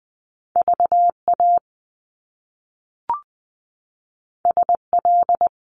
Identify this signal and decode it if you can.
Morse code, heard as a keyed tone, about 20 wpm, 700 Hz. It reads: VA SL
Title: Top Two Letter Combinations in Words 20wpm